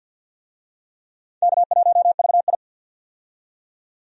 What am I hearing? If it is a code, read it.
Morse code, heard as a keyed tone, about 47 wpm, 700 Hz. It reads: K1VI
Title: USA Call Signs - 4 Characters - 50wpm